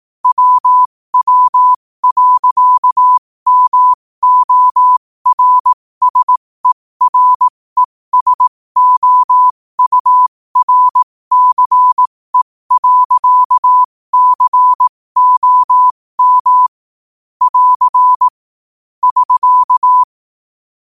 Morse code, heard as a keyed tone, about 18 wpm, spaced as for 15 wpm, 1000 Hz. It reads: WW.MORSERESOURCE.COM <AR> <SK>
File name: qotdfiles / 15wpm